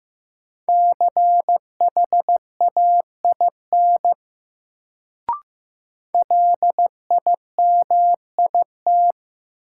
Morse code, heard as a keyed tone, about 15 wpm, 700 Hz. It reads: CHAIN LIMIT
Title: Sentences from Top 1500 Words - New Word Drill - 15wpm